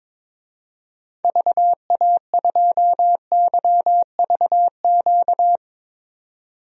Morse code, heard as a keyed tone, about 22 wpm, 700 Hz. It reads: VA2YVQ
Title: DXCC-100 Call Signs - 6 Characters - 22wpm